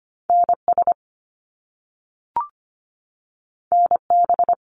Morse code, heard as a keyed tone, about 25 wpm, 700 Hz. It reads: NS NB
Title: Canadian Provinces and Territories 25wpm